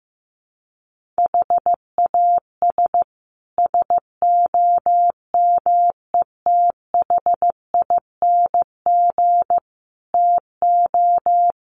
Morse code, heard as a keyed tone, about 15 wpm, 700 Hz. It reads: HAS SOMETHING TO
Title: Sets of 5 Words Encore 15wpm